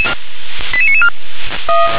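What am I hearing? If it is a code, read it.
Five-tone selective call (ZVEI), heard as quick runs of five tones, tones of about 70 ms. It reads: 88994
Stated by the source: ZVEI five-tone selective call received on an SDR.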